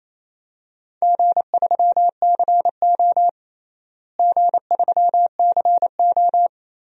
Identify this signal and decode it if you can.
Morse code, heard as a keyed tone, about 28 wpm, 700 Hz. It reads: G3CO G3CO
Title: DXCC-100 Call Signs - 4 Characters - Repeated 2x - 28wpm